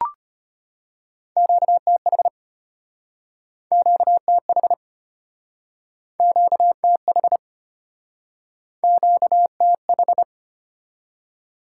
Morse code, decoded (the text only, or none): QTH QTH QTH QTH